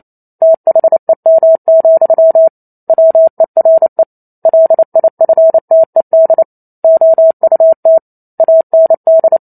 Morse code, heard as a keyed tone, about 28 wpm, 650 Hz. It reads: THEM, WERE LIFTED OUT AND
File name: Chapter 2 - 28 WPM